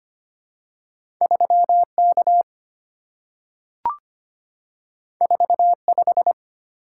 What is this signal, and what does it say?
Morse code, heard as a keyed tone, about 25 wpm, 700 Hz. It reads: 3K 45
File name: Two Letter-Number Combinations - 25wpm